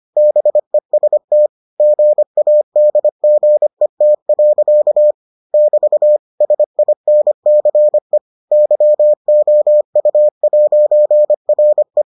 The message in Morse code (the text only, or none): BEST GADGET. <BT> SINCE YOU'RE